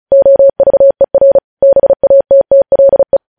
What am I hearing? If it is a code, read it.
Morse code, heard as a keyed tone, about 36 wpm, 550 Hz. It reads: OVER BATTLE
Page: News headlines in Morse code at 35 wpm